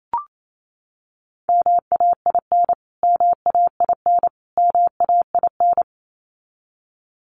Morse code, heard as a keyed tone, about 28 wpm, 700 Hz. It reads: MAIN MAIN MAIN
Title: Top 500 Words Repeated 28wpm